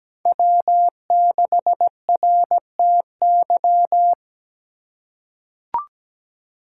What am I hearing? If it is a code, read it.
Morse code, heard as a keyed tone, about 17 wpm, 700 Hz. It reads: W6RTY
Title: USA Call Signs - 5 Characters - 3x Character Spacing - 17wpm